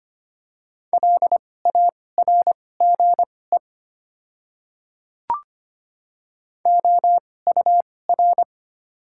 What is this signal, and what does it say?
Morse code, heard as a keyed tone, about 25 wpm, spaced as for 18 wpm, 700 Hz. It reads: LARGE OUR
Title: 2x Character Spacing - Top 200 Words - 25wpm